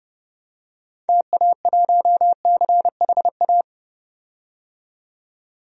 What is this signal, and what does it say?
Morse code, heard as a keyed tone, about 30 wpm, 700 Hz. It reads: TA1CHA
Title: DXCC-100 Call Signs - 6 Characters - 30wpm